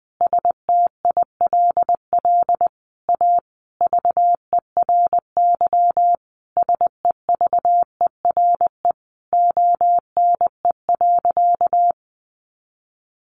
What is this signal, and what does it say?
Morse code, heard as a keyed tone, about 20 wpm, 700 Hz. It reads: STILL A VERY SEVERE ONE.
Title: Sentences from Top 1400 Words 20wpm